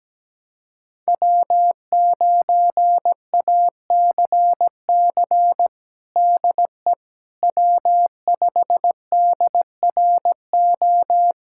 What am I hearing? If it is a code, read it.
Morse code, heard as a keyed tone, about 17 wpm, 700 Hz. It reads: W9ACC DE W5DRO
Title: USA Callsign DE Callsign - 5 Characters - 17wpm